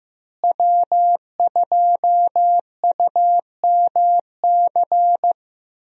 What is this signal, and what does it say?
Morse code, heard as a keyed tone, about 15 wpm, 700 Hz. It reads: W2UMC
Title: USA Call Signs - 5 Characters - Repeated 3x - 15wpm